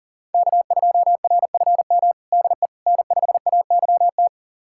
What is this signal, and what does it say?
Morse code, heard as a keyed tone, about 37 wpm, 700 Hz. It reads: K2RFM DE N5AYT
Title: USA Callsign DE Callsign - 5 Characters - 40wpm